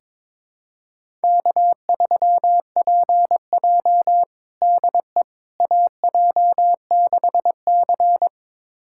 Morse code, heard as a keyed tone, about 22 wpm, 700 Hz. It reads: K3PJ DE AJ6C
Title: USA Callsign DE Callsign - 4 Characters - 22wpm